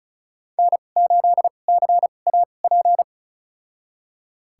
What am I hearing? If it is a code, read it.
Morse code, heard as a keyed tone, about 34 wpm, spaced as for 25 wpm, 700 Hz. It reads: N8CAP
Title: USA Call Signs - 5 Characters - 2x Character Spacing - 35wpm